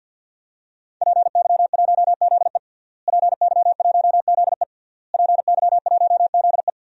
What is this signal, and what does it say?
Morse code, heard as a keyed tone, about 47 wpm, 700 Hz. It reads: PY1ZE PY1ZE PY1ZE
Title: DXCC-100 Call Signs - 5 Characters - Repeated 3x - 50wpm